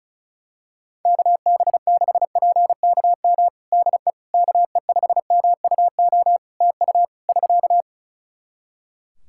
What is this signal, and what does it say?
Morse code, heard as a keyed tone, about 34 wpm, 700 Hz. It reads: KB6PKM DE KE5MUO TU <SK>
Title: Practice QSO - Exchange 4 - Ending - 35wpm